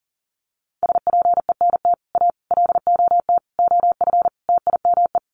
Morse code, heard as a keyed tone, about 40 wpm, 700 Hz. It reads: SPENT A LOT OF TIME
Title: Sets of 5 Words 40wpm